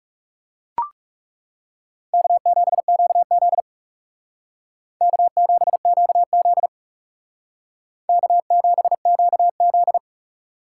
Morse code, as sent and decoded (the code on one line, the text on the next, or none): -.- --... --.- --.. / -.- --... --.- --.. / -.- --... --.- --..
K7QZ K7QZ K7QZ